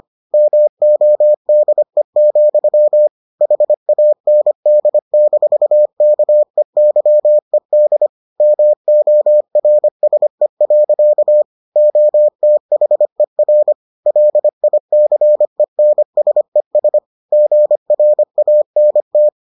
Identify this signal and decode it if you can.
Morse code, heard as a keyed tone, about 25 wpm, 600 Hz. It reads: MODE, HAND-KEYED MORSE. OTHER LICENSES GRANT